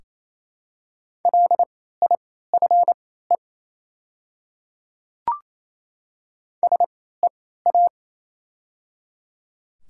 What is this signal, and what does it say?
Morse code, heard as a keyed tone, about 28 wpm, spaced as for 16 wpm, 690 Hz. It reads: LIFE SEA
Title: 3x Character Spacing - Top 500 Words - 28wpm